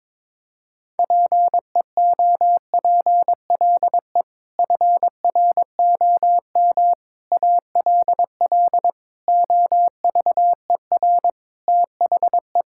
Morse code, heard as a keyed tone, about 22 wpm, 700 Hz. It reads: PEOPLE FROM ALL OVER THE
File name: practice / 22wpm / Sets of 6 Words - 22wpm